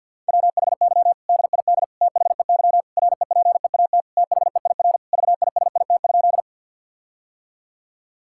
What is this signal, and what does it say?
Morse code, heard as a keyed tone, about 50 wpm, 700 Hz. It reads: WHY DID THEY REPEAT THEIR VISIT?
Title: Sentences from Top 1600 Words 50wpm